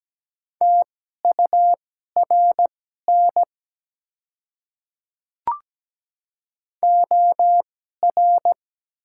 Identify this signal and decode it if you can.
Morse code, heard as a keyed tone, about 17 wpm, spaced as for 12 wpm, 700 Hz. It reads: TURN OR
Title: 2x Character Spacing - Top 200 Words - 17wpm